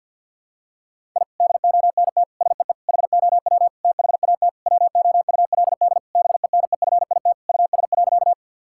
Morse code, heard as a keyed tone, about 51 wpm, 700 Hz. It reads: I DONT SEE HOW THAT WOULD BENEFIT US.
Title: Sentences from Top 1000 Words - Review Part 6 - 50wpm